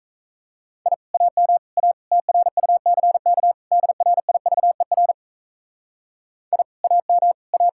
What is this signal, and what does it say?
Morse code, heard as a keyed tone, about 36 wpm, 700 Hz. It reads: I AM A TRUCK DRIVER I AM A